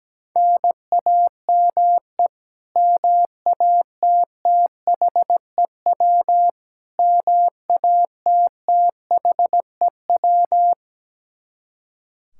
Morse code, heard as a keyed tone, about 17 wpm, 700 Hz. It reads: NAME MATTHEW MATTHEW
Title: K1USN SST QSO Practice 17wpm